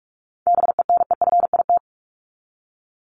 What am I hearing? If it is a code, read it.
Morse code, heard as a keyed tone, about 44 wpm, 700 Hz. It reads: BENEFIT
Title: Top 751-1000 Words 45wpm